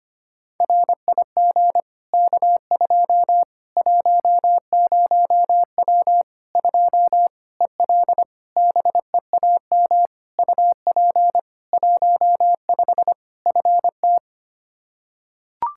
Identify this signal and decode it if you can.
Morse code, heard as a keyed tone, about 25 wpm, 700 Hz. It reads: RIG K2 10W 2 EL BEAM UP 15 FT
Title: Practice QSO - Exchange 2 - Info - 25wpm